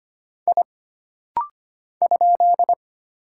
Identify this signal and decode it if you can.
Morse code, heard as a keyed tone, about 25 wpm, 700 Hz. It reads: I ?
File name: Single Character (Letter:Number:Punctuation:Prosign) - Rapid-Fire - 25wpm